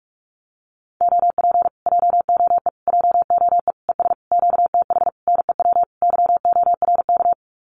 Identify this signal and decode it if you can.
Morse code, heard as a keyed tone, about 44 wpm, 700 Hz. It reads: OP JOE JOE ES QTH NEW YORK
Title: Common QSO Phrases Encore 45wpm